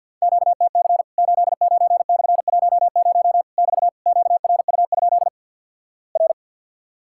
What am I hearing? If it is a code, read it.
Morse code, heard as a keyed tone, about 50 wpm, 700 Hz. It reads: QTC 89/10 <BT> QRU? R
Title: Worked All Europe DX Contest - QTC Exchange - 50wpm